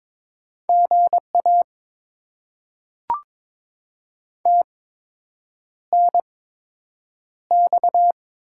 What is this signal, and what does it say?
Morse code, heard as a keyed tone, about 22 wpm, spaced as for 8 wpm, 700 Hz. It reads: GA TNX
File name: Top 20 QSO Elements - 8x Character Spacing - 22wpm